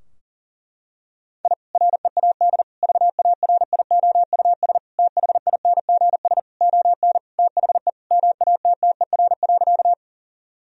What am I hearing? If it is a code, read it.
Morse code, heard as a keyed tone, about 40 wpm, 700 Hz. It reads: I READ VARIOUS THINGS ON THE MATTER.
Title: Sentences from Top 1000 Words - Review Part 4 - 40wpm